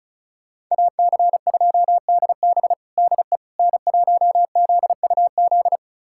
Morse code, heard as a keyed tone, about 36 wpm, 700 Hz. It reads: AC2DB DE N1ZUZ
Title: USA Callsign DE Callsign - 5 Characters - 35wpm